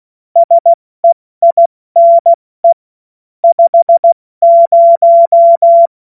Morse code, heard as a keyed tone, about 16 wpm, 680 Hz. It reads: SEINE 50
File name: Morsekurs / Witze / Witz 15. November 2022 70 F80